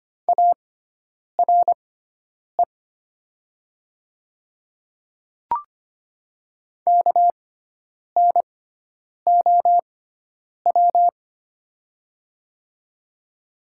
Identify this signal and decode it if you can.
Morse code, heard as a keyed tone, about 25 wpm, spaced as for 9 wpm, 700 Hz. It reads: ARE KNOW
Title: 6x Character Spacing - Top 100 Words - 25wpm